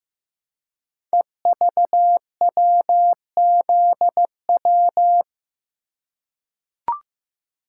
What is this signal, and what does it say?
Morse code, heard as a keyed tone, about 15 wpm, 700 Hz. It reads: EVWZW
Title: Code Groups - 5 Letter:Numbers - 15wpm